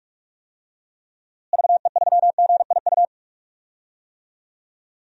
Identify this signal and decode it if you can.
Morse code, heard as a keyed tone, about 41 wpm, 700 Hz. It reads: VE3GIU
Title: DXCC-100 Call Signs - 6 Characters - Repeated 3x - 45wpm